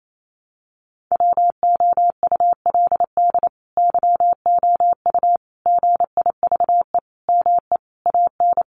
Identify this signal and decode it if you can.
Morse code, heard as a keyed tone, about 28 wpm, 700 Hz. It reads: WOULD YOU GIVE ME AN
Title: Sentences from Top 1000 Words 28wpm